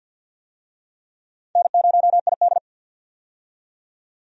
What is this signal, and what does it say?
Morse code, heard as a keyed tone, about 52 wpm, 700 Hz. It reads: N0ID
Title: USA Call Signs - 4 Characters - 50wpm